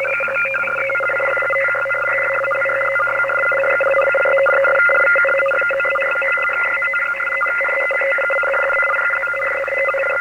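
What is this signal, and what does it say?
Two-tone teleprinter signal, about 850 Hz shift.
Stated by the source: Shortwave reception recorded on a Kenwood TS-520 amateur radio to a Fostex X-15 four-track cassette recorder.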